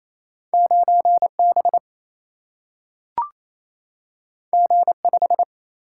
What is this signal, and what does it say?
Morse code, heard as a keyed tone, about 28 wpm, 700 Hz. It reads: 9B G5
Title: Single-Letter Single-Number 28wpm